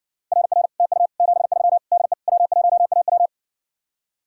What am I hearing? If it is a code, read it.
Morse code, heard as a keyed tone, about 60 wpm, 700 Hz. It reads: RR TU 73 DE W1AW